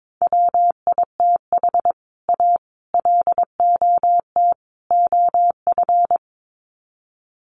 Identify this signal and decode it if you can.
Morse code, heard as a keyed tone, about 22 wpm, 700 Hz. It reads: WITH A LOT OF
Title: Sets of 4 Words 22wpm